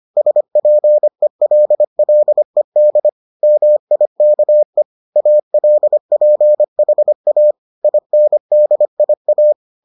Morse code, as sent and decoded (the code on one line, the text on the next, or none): ... .--. .-.. .-.. -.. / -- .. -.- . / .- .-.. .--. .... .- / .. -. -.. .. .-
SPELLED MIKE ALPHA INDIA